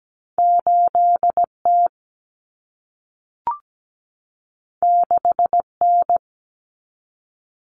Morse code, heard as a keyed tone, about 17 wpm, 700 Hz. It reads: 8T 6N
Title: Common Cut Numbers - 2-Digit 17wpm